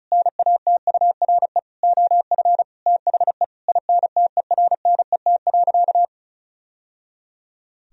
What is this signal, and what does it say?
Morse code, heard as a keyed tone, about 36 wpm, 700 Hz. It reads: NATURE OF THE INTERNET.
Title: Sentences from Top 1000 Words - Review Part 4 - 35wpm